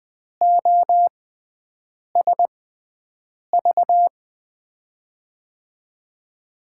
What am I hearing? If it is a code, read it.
Morse code, heard as a keyed tone, about 20 wpm, spaced as for 7 wpm, 700 Hz. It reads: OSV
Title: USA Call Signs - 5 Characters - 6x Character Spacing - 20wpm